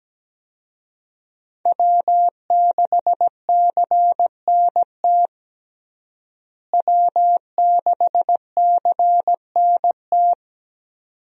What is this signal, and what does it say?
Morse code, heard as a keyed tone, about 17 wpm, 700 Hz. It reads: W6CNT W6CNT
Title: USA Call Signs - 5 Characters - Repeated 2x - 17wpm